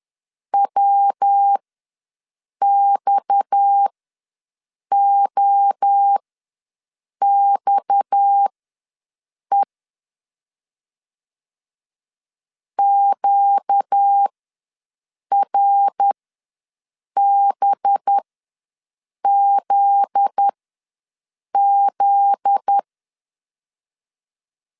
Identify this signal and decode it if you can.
Morse code, heard as a keyed tone, about 10 wpm, spaced as for 6 wpm, 800 Hz. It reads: WXOXE QRBZZ
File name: LEZIONI / 7 GF9Z / 7 GF9Z ws 10wpm 3919